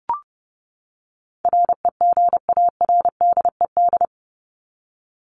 Morse code, heard as a keyed tone, about 30 wpm, 700 Hz. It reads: REGARDED
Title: ED Suffix - Top 100 Words with Base Words 30wpm